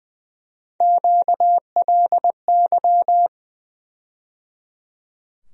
Morse code, heard as a keyed tone, about 19 wpm, 700 Hz. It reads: QLY
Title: USA Call Sign Postfixes - 3 Letters 20wpm